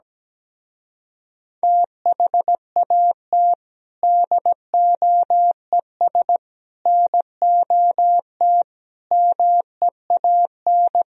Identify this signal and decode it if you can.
Morse code, heard as a keyed tone, about 17 wpm, 700 Hz. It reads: THAT DOES NOT MEAN
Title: Speed-Racing - Sets of 4 Words 17wpm